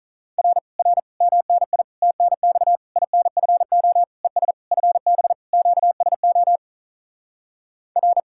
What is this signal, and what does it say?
Morse code, heard as a keyed tone, about 48 wpm, 700 Hz. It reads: R R MNI TNX INFO ES FB QSO R